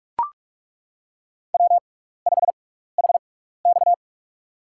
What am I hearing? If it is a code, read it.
Morse code, heard as a keyed tone, about 42 wpm, spaced as for 15 wpm, 700 Hz. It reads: W5HX
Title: USA Call Signs - 6x Character Spacing - 45wpm